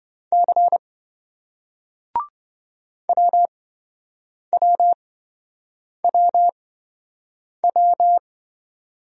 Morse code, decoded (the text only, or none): C WWWW